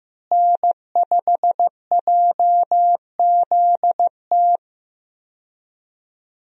Morse code, heard as a keyed tone, about 15 wpm, 700 Hz. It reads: N5JZT
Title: USA Call Signs - 5 Characters - Repeated 2x - 15wpm